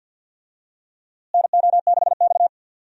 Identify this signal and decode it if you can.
Morse code, heard as a keyed tone, about 47 wpm, 700 Hz. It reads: NO6X
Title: USA Call Signs - 4 Characters - Repeated 3x - 50wpm